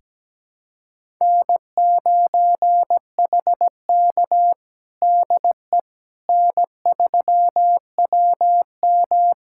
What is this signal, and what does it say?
Morse code, heard as a keyed tone, about 17 wpm, 700 Hz. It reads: N9HK DE N3WM
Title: USA Callsign DE Callsign - 4 Characters - 17wpm